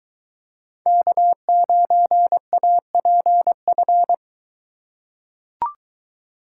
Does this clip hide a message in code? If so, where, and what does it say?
Yes: Morse code from 0.9 to 4.1 s, heard as a keyed tone, about 22 wpm, 700 Hz. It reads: K9APF